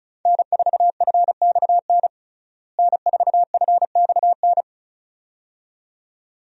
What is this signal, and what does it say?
Morse code, heard as a keyed tone, about 34 wpm, 700 Hz. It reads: N4FXN N4FXN
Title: USA Call Signs - 5 Characters - Repeated 2x - 35wpm